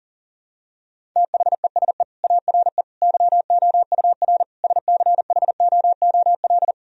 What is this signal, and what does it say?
Morse code, heard as a keyed tone, about 37 wpm, 700 Hz. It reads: THESE ARE YOUR SCHOOL
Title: Sentences from Top 1000 Words - Review Part 1 - 40wpm